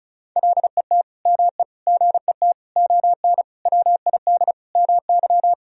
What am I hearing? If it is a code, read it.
Morse code, heard as a keyed tone, about 36 wpm, 700 Hz. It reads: LET ME GET ON WID MY
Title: Sentences from Top 300 Words and Abbreviations 35wpm